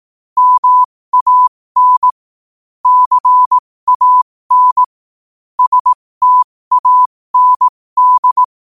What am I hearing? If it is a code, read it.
Morse code, heard as a keyed tone, about 18 wpm, spaced as for 15 wpm, 1000 Hz. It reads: MAN CAN STAND
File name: qotdfiles / 15wpm